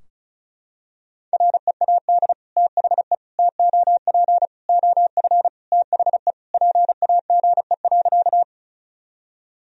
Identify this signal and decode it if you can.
Morse code, heard as a keyed tone, about 36 wpm, 700 Hz. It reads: READ THE TOP OF THE PAGE.
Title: Sentences from Top 1000 Words - Review Part 2 - 35wpm